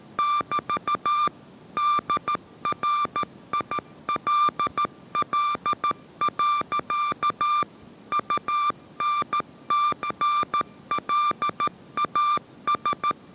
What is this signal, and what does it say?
Morse code, heard as a keyed tone, about 14 wpm, 1180 Hz. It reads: <BT> DRILL. UNTNLAS
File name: 14wpm